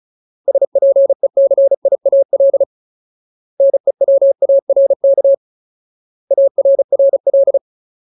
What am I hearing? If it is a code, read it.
Morse code, heard as a keyed tone, about 34 wpm, 550 Hz. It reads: SPECIAL NEWARK ARRL
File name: Ultimate Word List / 35WPM Ultimate